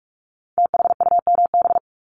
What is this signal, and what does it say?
Morse code, heard as a keyed tone, about 46 wpm, 700 Hz. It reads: THUMB
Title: Multi-letter Phonemes Repeated 3x 45wpm